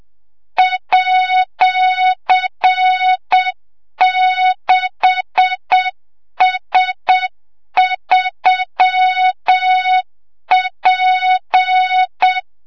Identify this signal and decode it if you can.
Morse code, heard as a keyed tone, about 7 wpm, 3700 Hz, with noise. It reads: @6S3P
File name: Morze-HA2MN / hang / 25 GYAKORLAT